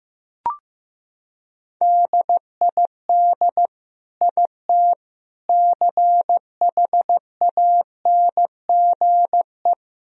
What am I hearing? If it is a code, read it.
Morse code, heard as a keyed tone, about 15 wpm, 700 Hz. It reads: DID IT CHANGE
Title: Sentences from Top 1000 Words - Review Part 1 - 15wpm